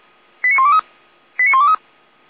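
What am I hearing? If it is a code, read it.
Five-tone selective call (ZVEI), heard as quick runs of five tones, tones of about 70 ms. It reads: 89123 89123
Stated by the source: ZVEI five-tone selective call received on an SDR.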